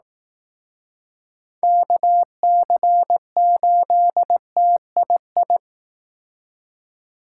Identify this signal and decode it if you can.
Morse code, heard as a keyed tone, about 18 wpm, 700 Hz. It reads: KC8TII
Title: Speed Racing - USA 2x3 Call Signs 15wpm